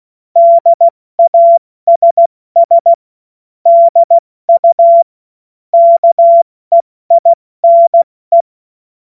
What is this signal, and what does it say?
Morse code, heard as a keyed tone, about 16 wpm, 680 Hz. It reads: DASS DU KEINE
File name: Morsekurs / Witze / Witz 05 März 2024 70 F80